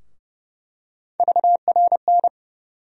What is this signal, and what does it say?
Morse code, heard as a keyed tone, about 31 wpm, 700 Hz. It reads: VRN
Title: USA Call Sign Postfixes - 3 Letters 30wpm